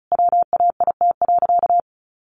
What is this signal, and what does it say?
Morse code, heard as a keyed tone, about 34 wpm, 700 Hz. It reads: WAIT.
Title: Sentences from Top 800 Words 35wpm